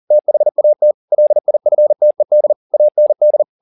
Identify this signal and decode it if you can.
Morse code, heard as a keyed tone, about 40 wpm, 600 Hz. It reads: THAT LIFTED AND
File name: the old man and the sea 2 40wpm